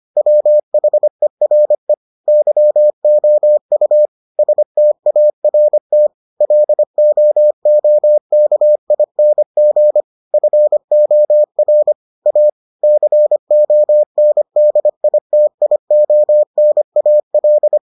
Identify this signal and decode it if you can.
Morse code, heard as a keyed tone, about 25 wpm, 600 Hz. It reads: WHERE YOU START LOOKING FOR A CONDITIONAL